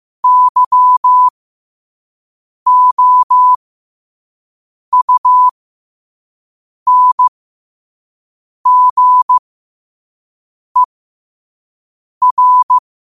Morse code, heard as a keyed tone, about 15 wpm, spaced as for 5 wpm, 1000 Hz. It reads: YOUNGER
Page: New quotes every day in morse code at 5 Words per minute.